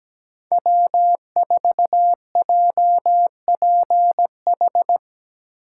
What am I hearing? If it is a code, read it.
Morse code, heard as a keyed tone, about 17 wpm, 700 Hz. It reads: W4JPH
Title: K1USN SST QSO Practice - 17wpm